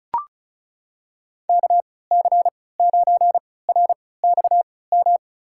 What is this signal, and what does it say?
Morse code, heard as a keyed tone, about 36 wpm, spaced as for 20 wpm, 700 Hz. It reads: KC9RXM
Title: USA Call Signs - 6 Characters - 3x Character Spacing - 35wpm